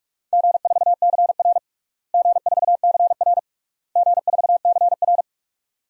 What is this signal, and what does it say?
Morse code, heard as a keyed tone, about 46 wpm, 700 Hz. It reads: G4CR G4CR G4CR
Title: DXCC-100 Call Signs - 4 Characters - Repeated 3x - 45wpm